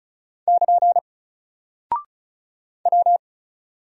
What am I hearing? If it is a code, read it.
Morse code, heard as a keyed tone, about 36 wpm, 700 Hz. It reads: <KN> W
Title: Single Character (Letter:Number:Punctuation:Prosign) Alt 35wpm